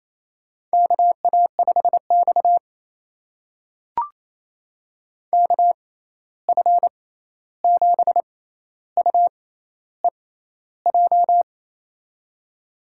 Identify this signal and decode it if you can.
Morse code, heard as a keyed tone, about 28 wpm, spaced as for 13 wpm, 700 Hz. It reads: KA5X KF7UEJ
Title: USA Call Signs - 6x Character Spacing - 28wpm